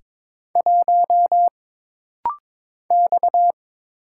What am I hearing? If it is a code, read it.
Morse code, heard as a keyed tone, about 22 wpm, 700 Hz. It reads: 1 X
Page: This rapid-fire timing is geared to push your ICR (Instant Character Recognition) skills.